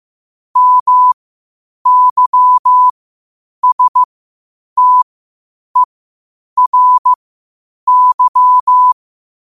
Morse code, heard as a keyed tone, about 15 wpm, spaced as for 8 wpm, 1000 Hz. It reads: MYSTERY
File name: qotdfiles / 8wpm